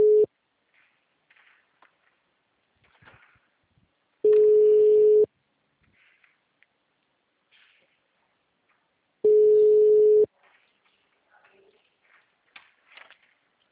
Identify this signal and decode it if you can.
Ringback tone (Europe, 425 Hz).